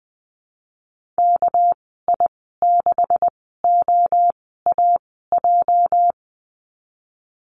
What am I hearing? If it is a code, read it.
Morse code, heard as a keyed tone, about 20 wpm, spaced as for 15 wpm, 700 Hz. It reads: KI6OAJ
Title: 207 Lesson 33g - Letter j - Call Signs 20wpm